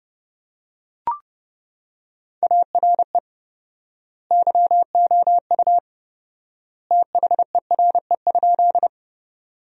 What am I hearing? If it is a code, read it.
Morse code, heard as a keyed tone, about 32 wpm, 700 Hz. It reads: ARE YOU THERE?
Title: 4x Word-Spacing - Sentences from Top 100 Words - 30wpm